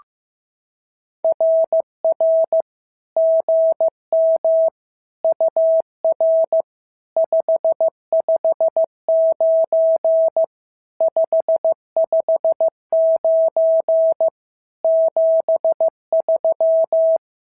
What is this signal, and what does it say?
Morse code, heard as a keyed tone, about 14 wpm, 650 Hz. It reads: RR GM UR 559 559 73
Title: SOTA QSO Practice 15wpm